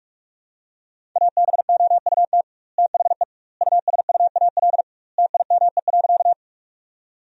Morse code, heard as a keyed tone, about 42 wpm, 700 Hz. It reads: ABOUT THE USUAL TIME.
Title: 2x Word-Spacing - Sentences from Top 400 Words - 45wpm